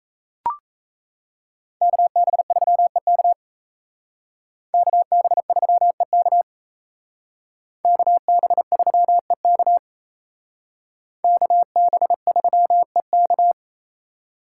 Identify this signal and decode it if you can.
Morse code, heard as a keyed tone, about 33 wpm, 700 Hz. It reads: KB3EK KB3EK KB3EK KB3EK
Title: Speed Racing - USA 2x2 Call Signs 28wpm